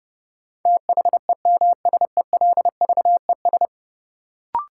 Morse code, heard as a keyed tone, about 30 wpm, 700 Hz. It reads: THEMSELVES
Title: Sentences from Top 800 Words - New Word Drill - 30wpm